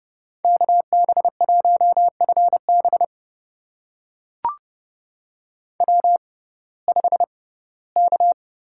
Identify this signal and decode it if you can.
Morse code, heard as a keyed tone, about 29 wpm, spaced as for 18 wpm, 700 Hz. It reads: KB1FB W5K
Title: USA Call Signs - 5 Characters - 6x Character Spacing - 30wpm